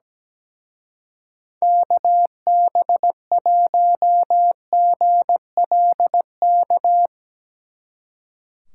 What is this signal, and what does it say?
Morse code, heard as a keyed tone, about 19 wpm, 700 Hz. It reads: KB1GLK
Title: Speed Racing - USA 2x3 Call Signs 17wpm